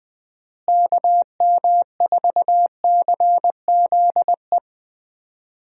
Morse code, heard as a keyed tone, about 20 wpm, 700 Hz. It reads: KM4CZE
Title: USA Call Signs - 6 Characters - Repeated 3x - 20wpm